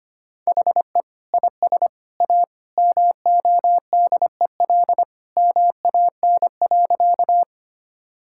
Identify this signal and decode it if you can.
Morse code, heard as a keyed tone, about 25 wpm, 700 Hz. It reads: HE IS A MODEL MAN.
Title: Sentences from Top 1000 Words - Review Part 5 - 25wpm